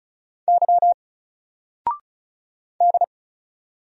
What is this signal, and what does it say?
Morse code, heard as a keyed tone, about 35 wpm, 700 Hz. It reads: Y D
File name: Single-Letter-Number-35wpm